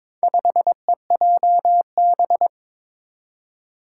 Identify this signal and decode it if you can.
Morse code, heard as a keyed tone, about 22 wpm, 700 Hz. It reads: 5EJB
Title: Four Challenging Letter-Number Combinations 22wpm